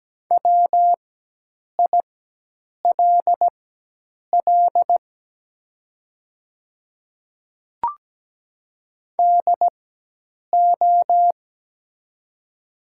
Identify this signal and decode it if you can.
Morse code, heard as a keyed tone, about 17 wpm, spaced as for 8 wpm, 700 Hz. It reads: WILL DO
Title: 4x Character Spacing - Top 100 Words - 17wpm